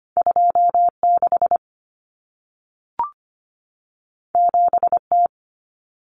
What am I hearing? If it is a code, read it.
Morse code, heard as a keyed tone, about 25 wpm, 700 Hz. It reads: 26 7T
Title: Common Cut Numbers - 2-Digit 25wpm